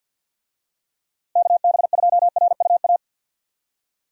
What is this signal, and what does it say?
Morse code, heard as a keyed tone, about 50 wpm, 700 Hz. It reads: KB2RUA
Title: USA Call Signs - 6 Characters - 50wpm